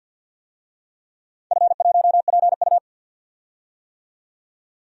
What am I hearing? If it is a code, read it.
Morse code, heard as a keyed tone, about 53 wpm, 700 Hz. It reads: F1PU
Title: DXCC-100 Call Signs - 4 Characters - 50wpm